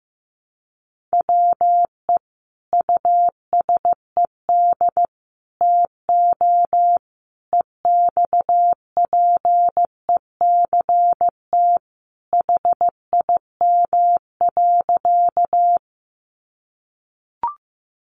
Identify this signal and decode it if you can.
Morse code, heard as a keyed tone, about 15 wpm, 700 Hz. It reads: WE USED TO EXPECT HIM.
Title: Sentences from Top 1100 Words 15wpm